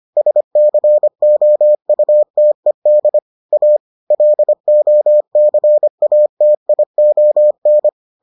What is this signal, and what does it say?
Morse code, heard as a keyed tone, about 25 wpm, 600 Hz. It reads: SCOUTED A LOCATION